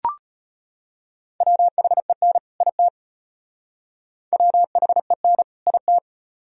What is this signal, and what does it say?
Morse code, heard as a keyed tone, about 32 wpm, 700 Hz. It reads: WHEN IT WHEN IT